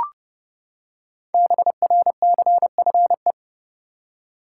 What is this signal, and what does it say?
Morse code, heard as a keyed tone, about 30 wpm, 700 Hz. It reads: BRCFE
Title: Code Groups - 5 Letters - 30wpm